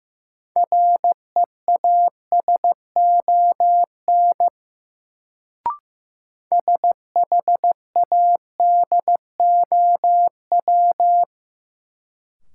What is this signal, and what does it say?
Morse code, heard as a keyed tone, about 15 wpm, 700 Hz. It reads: REASON SHADOW
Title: Sets of 5 Words Encore - New Word Drill - 15wpm